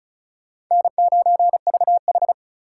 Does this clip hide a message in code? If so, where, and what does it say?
Yes: Morse code from 0.7 to 2.3 s, heard as a keyed tone, about 46 wpm, 700 Hz. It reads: N9VH